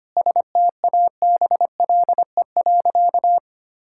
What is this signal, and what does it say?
Morse code, heard as a keyed tone, about 25 wpm, 700 Hz. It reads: STABLE.
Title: Sentences from Top 1600 Words 25wpm